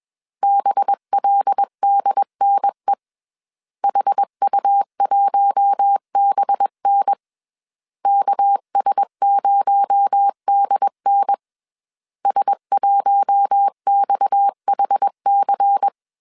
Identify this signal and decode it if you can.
Morse code, heard as a keyed tone, about 21 wpm, 800 Hz. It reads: BLDNE HU1BN KS0DN S1XHC
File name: CW / DET / 1 6 1500C S A 20wpm 2048